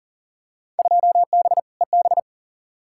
Morse code, heard as a keyed tone, about 40 wpm, 700 Hz. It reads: 2B EB
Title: Field Day Exchanges - 40wpm